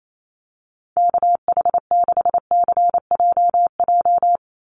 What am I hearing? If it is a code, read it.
Morse code, heard as a keyed tone, about 28 wpm, 700 Hz. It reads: KH6CJJ
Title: Call Signs Repeated 2x - 28wpm